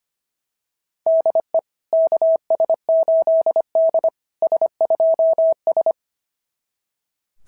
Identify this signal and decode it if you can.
Morse code, heard as a keyed tone, about 25 wpm, 690 Hz. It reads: DE KS8D S2S
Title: SOTA QSO Practice 25wpm